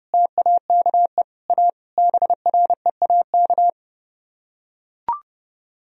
Morse code, heard as a keyed tone, about 30 wpm, 700 Hz. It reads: TAKE A BREAK